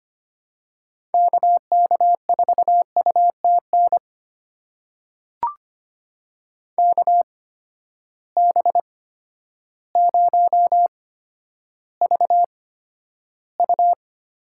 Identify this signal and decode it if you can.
Morse code, heard as a keyed tone, about 25 wpm, spaced as for 11 wpm, 700 Hz. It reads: KK4UTN KB0VU
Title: USA Call Signs - 6 Characters - 8x Character Spacing - 25wpm